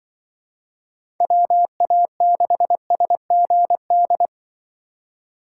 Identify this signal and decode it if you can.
Morse code, heard as a keyed tone, about 24 wpm, 700 Hz. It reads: WA6SGD
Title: Speed Racing - USA Call Signs 20wpm